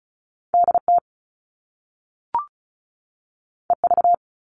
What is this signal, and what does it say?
Morse code, heard as a keyed tone, about 36 wpm, 690 Hz. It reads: DT EV
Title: All Cut Numbers - 2-Digit 35wpm